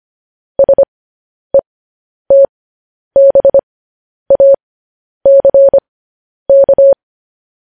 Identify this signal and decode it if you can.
Morse code, heard as a keyed tone, about 25 wpm, spaced as for 10 wpm, 550 Hz. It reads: SETBACK
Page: news-headlines-in-morse-code-at-10-wpm